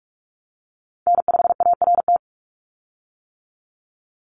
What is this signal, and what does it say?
Morse code, heard as a keyed tone, about 46 wpm, 690 Hz. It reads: N5ART
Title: USA Call Signs 45wpm